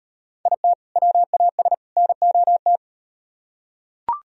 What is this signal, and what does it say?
Morse code, heard as a keyed tone, about 38 wpm, 700 Hz. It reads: IT WAS NOT